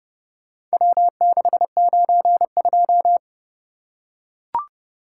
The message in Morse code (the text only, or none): W692